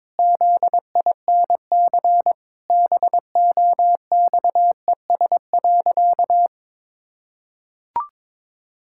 Morse code, heard as a keyed tone, about 22 wpm, 700 Hz. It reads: ZINC BOXES.
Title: Pangram Challenge v2 - 22wpm